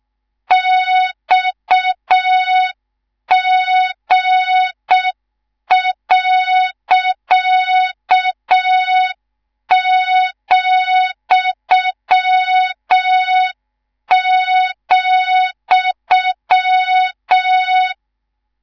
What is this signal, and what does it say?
Morse code, heard as a keyed tone, about 6 wpm, 3700 Hz. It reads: XG.,,
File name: Morze-HA2MN / hang / 19 GYAKORLAT